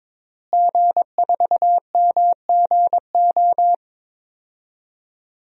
Morse code, heard as a keyed tone, about 22 wpm, 700 Hz. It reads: G4MGO
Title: DXCC-100 Call Signs - 5 Characters - Repeated 2x - 22wpm